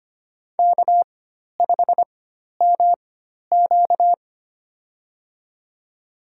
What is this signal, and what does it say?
Morse code, heard as a keyed tone, about 25 wpm, spaced as for 12 wpm, 700 Hz. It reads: K5MQ
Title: USA Call Signs - 4x Character Spacing - 25wpm